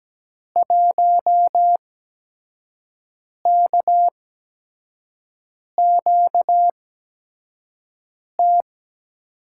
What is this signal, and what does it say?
Morse code, heard as a keyed tone, about 17 wpm, spaced as for 5 wpm, 700 Hz. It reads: 1KQT
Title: USA Call Signs - 5 Characters - 8x Character Spacing - 17wpm